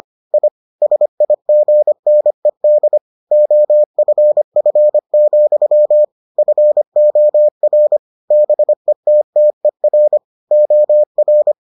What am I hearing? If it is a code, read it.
Morse code, heard as a keyed tone, about 25 wpm, 600 Hz. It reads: I SIGNED OFF, FOR BETTER OR